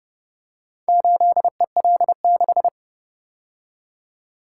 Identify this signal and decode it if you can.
Morse code, heard as a keyed tone, about 29 wpm, 700 Hz. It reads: 8EL6